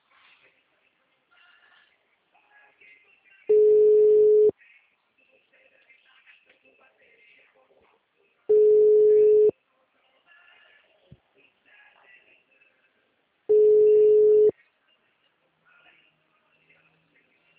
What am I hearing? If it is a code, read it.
Ringback tone (Europe, 425 Hz), with speech.
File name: out- -2006- - - .10388